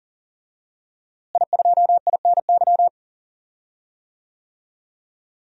Morse code, heard as a keyed tone, about 39 wpm, 700 Hz. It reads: I2INY
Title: DXCC-100 Call Signs - 5 Characters - Repeated 3x - 40wpm